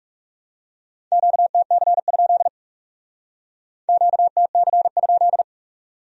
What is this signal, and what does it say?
Morse code, heard as a keyed tone, about 37 wpm, 700 Hz. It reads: QTC? QTC?
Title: Speed-Racing Q-Codes - 30wpm